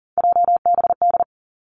Morse code, heard as a keyed tone, about 40 wpm, 700 Hz. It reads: JBD
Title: Three Challenging Letter-Number Combinations 40wpm